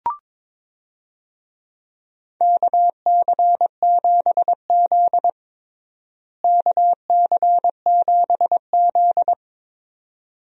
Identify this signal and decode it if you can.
Morse code, heard as a keyed tone, about 22 wpm, 700 Hz. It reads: KC7Z KC7Z
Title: USA Call Signs - 4 Characters - Repeated 2x - 22wpm